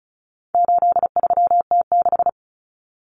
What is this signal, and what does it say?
Morse code, heard as a keyed tone, about 34 wpm, 700 Hz. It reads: 83T6
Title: Common Cut Numbers - 4-Digit 35wpm